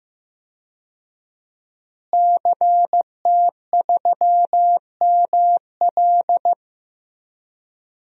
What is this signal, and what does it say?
Morse code, heard as a keyed tone, about 15 wpm, 700 Hz. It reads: CT3ML
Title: DXCC-100 Call Signs - 5 Characters - Repeated 3x - 15wpm